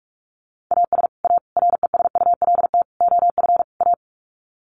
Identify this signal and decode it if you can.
Morse code, heard as a keyed tone, about 44 wpm, 700 Hz. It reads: AS A RESULT OF A
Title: Sets of 5 Words Encore 45wpm